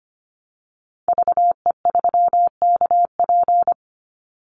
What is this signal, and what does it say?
Morse code, heard as a keyed tone, about 25 wpm, 700 Hz. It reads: VE3KP
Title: Call Signs - 25wpm